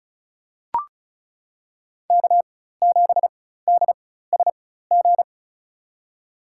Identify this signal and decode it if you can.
Morse code, heard as a keyed tone, about 36 wpm, spaced as for 16 wpm, 700 Hz. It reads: K7DSG
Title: USA Call Signs - 5 Characters - 4x Character Spacing - 35wpm